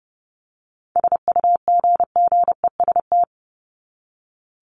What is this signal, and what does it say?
Morse code, heard as a keyed tone, about 30 wpm, 700 Hz. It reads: SUGGEST
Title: ED Suffix - Top 100 Words with Base Words 30wpm